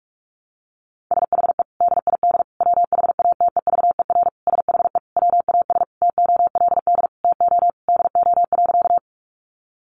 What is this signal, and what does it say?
Morse code, heard as a keyed tone, about 44 wpm, 700 Hz. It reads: SHE DID WHATEVER SHE WAS TOLD TO DO.
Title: Sentences from Top 1100 Words 45wpm